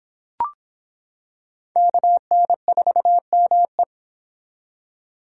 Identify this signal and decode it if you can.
Morse code, heard as a keyed tone, about 25 wpm, 700 Hz. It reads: KN4ME